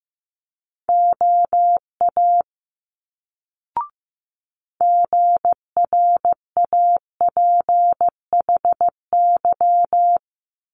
Morse code, heard as a keyed tone, about 15 wpm, 700 Hz. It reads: OA GRAPHY
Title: Multi-letter Phonemes Quick Fire 15wpm